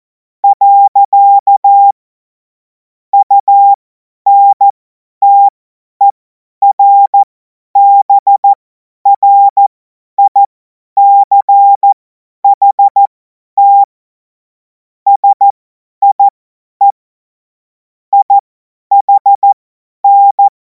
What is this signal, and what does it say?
Morse code, heard as a keyed tone, about 14 wpm, spaced as for 10 wpm, 800 Hz. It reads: . UNTERBRICHT SIE IHN
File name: Morsekurs / Witze / Witz 14. Juli 2026 50 F70